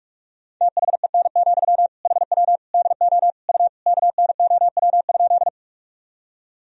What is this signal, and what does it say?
Morse code, heard as a keyed tone, about 46 wpm, 700 Hz. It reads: THEN, HW DO U KNOW?
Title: Sentences from Top 100 Words and Abbreviations 45wpm